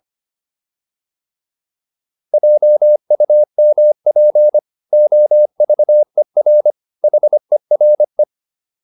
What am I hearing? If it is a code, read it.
Morse code, heard as a keyed tone, about 25 wpm, 600 Hz. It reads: JUMP OVER HERE
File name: PR405 25